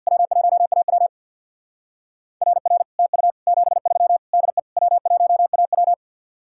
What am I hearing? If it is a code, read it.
Morse code, heard as a keyed tone, about 61 wpm, 700 Hz. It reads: W1AW RR TU 73 DE W1AW
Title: Speed-Racing Common QSO Phrases Encore - 50wpm